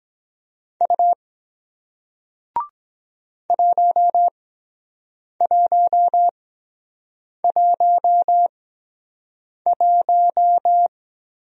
Morse code, heard as a keyed tone, about 23 wpm, spaced as for 7 wpm, 700 Hz. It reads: U 1111